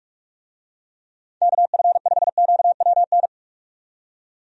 Morse code, heard as a keyed tone, about 44 wpm, 700 Hz. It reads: KF5QWN
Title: USA Call Signs - 6 Characters - 45wpm